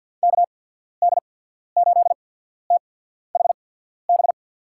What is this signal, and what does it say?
Morse code, heard as a keyed tone, about 52 wpm, spaced as for 14 wpm, 700 Hz. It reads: KD8THB
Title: USA Call Signs - 6 Characters - 8x Character Spacing - 50wpm